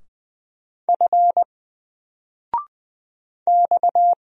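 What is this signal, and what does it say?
Morse code, heard as a keyed tone, about 20 wpm, 700 Hz. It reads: F X
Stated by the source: Single letter and number Morse drill at 20 wpm, from N4LCD.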